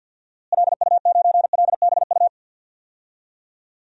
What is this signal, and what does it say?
Morse code, heard as a keyed tone, about 47 wpm, 700 Hz. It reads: LU9LBU
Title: DXCC-100 Call Signs - 6 Characters - Repeated 2x - 50wpm